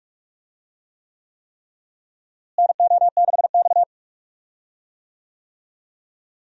Morse code, heard as a keyed tone, about 45 wpm, 700 Hz. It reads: NO6X
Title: USA Call Signs - 4 Characters - Repeated 3x - 45wpm